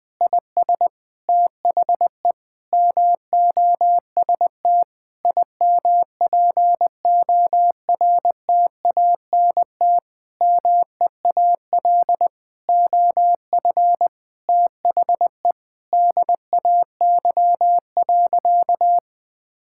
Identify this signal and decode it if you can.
Morse code, heard as a keyed tone, about 20 wpm, 700 Hz. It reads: IS THE MOST IMPORTANT MEAL OF THE DAY.
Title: Sentences with 9-Word Phrases - 20wpm